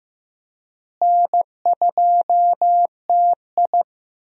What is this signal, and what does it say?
Morse code, heard as a keyed tone, about 15 wpm, 700 Hz. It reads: N2TI
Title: USA Call Signs - 4 Characters - Repeated 2x - 15wpm